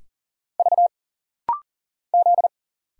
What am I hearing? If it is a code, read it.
Morse code, heard as a keyed tone, about 40 wpm, 700 Hz. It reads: V Z
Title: Single Letters - Rapid-Fire 40wpm